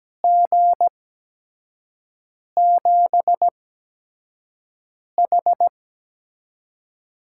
Morse code, heard as a keyed tone, about 17 wpm, spaced as for 5 wpm, 700 Hz. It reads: G7H
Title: USA Call Signs - 6 Characters - 8x Character Spacing - 17wpm